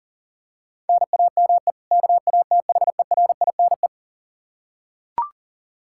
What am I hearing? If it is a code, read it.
Morse code, heard as a keyed tone, about 40 wpm, 700 Hz. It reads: NAME KATHERINE
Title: Practice QSO - Exchange 1 - Beginning - 40wpm